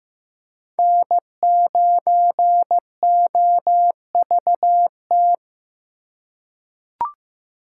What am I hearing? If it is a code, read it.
Morse code, heard as a keyed tone, about 16 wpm, 700 Hz. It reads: N9OVT
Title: USA Call Signs - 4x Character Spacing - 15wpm